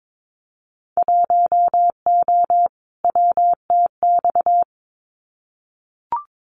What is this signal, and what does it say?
Morse code, heard as a keyed tone, about 22 wpm, 700 Hz. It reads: 1O WTX
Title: Winter Field Day Exchanges Encore 22wpm